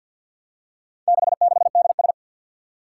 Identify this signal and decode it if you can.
Morse code, heard as a keyed tone, about 52 wpm, 700 Hz. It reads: 66DS
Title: Four Challenging Letter-Number Combinations 50wpm